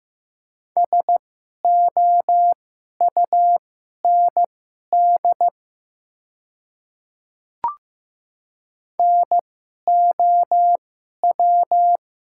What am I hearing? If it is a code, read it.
Morse code, heard as a keyed tone, about 15 wpm, spaced as for 11 wpm, 700 Hz. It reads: SOUND NOW
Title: 2x Character Spacing - Top 100 Words - 15wpm